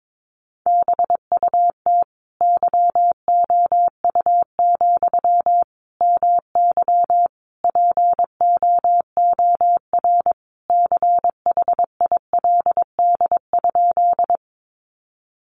Morse code, heard as a keyed tone, about 22 wpm, 700 Hz. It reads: BUT YOU, MY POOR CHILD?
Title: Sentences from Top 1000 Words 22wpm